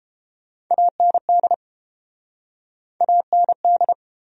Morse code, heard as a keyed tone, about 32 wpm, 700 Hz. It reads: AND AND